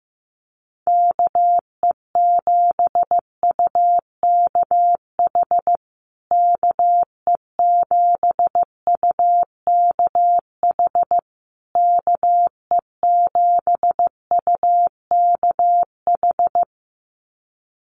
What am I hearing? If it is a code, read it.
Morse code, heard as a keyed tone, about 14 wpm, 700 Hz. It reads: KE7UKH KE7UKH KE7UKH
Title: USA Call Signs Repeated 3x 15wpm